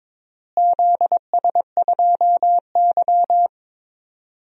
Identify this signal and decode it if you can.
Morse code, heard as a keyed tone, about 22 wpm, 700 Hz. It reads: ZS2Y
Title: DXCC-100 Call Signs - 4 Characters - Repeated 2x - 22wpm